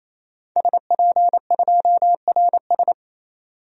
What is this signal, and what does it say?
Morse code, heard as a keyed tone, about 28 wpm, 700 Hz. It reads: SP2RS
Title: DXCC-100 Call Signs - 5 Characters - 28wpm